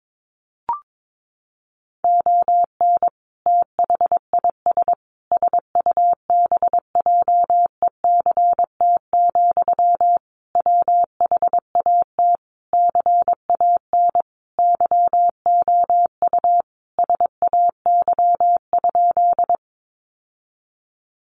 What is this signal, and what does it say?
Morse code, heard as a keyed tone, about 22 wpm, 700 Hz. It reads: ON THIS SUBJECT, WHAT CAN YOU SAY?
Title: Sentences from Top 800 Words 22wpm